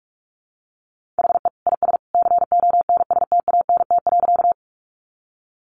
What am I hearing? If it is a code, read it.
Morse code, heard as a keyed tone, about 44 wpm, 700 Hz. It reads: HE IS CONSTANT.
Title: Sentences from Top 1500 Words 45wpm